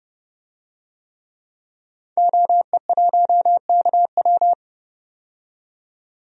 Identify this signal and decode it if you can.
Morse code, heard as a keyed tone, about 30 wpm, 700 Hz. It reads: OE1KW
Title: DXCC-100 Call Signs - 5 Characters - 30wpm